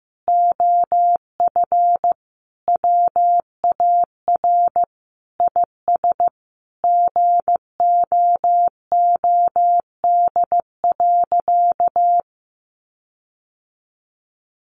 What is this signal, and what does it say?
Morse code, heard as a keyed tone, about 15 wpm, 700 Hz. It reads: OF WAR IS GOOD.
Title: Sentences from Top 1300 Words 15wpm